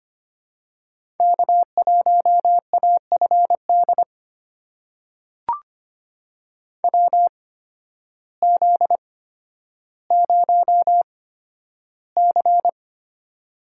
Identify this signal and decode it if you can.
Morse code, heard as a keyed tone, about 25 wpm, spaced as for 12 wpm, 700 Hz. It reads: K1AFD WZ0C
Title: USA Call Signs - 8x Character Spacing - 25wpm